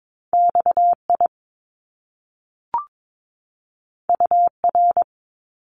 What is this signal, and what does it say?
Morse code, heard as a keyed tone, about 22 wpm, 700 Hz. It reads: XI UR
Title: Top Two Letter Combinations in Words 22wpm